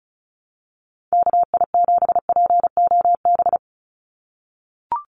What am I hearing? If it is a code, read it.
Morse code, heard as a keyed tone, about 34 wpm, 700 Hz. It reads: KI7POB
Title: USA Call Signs 35wpm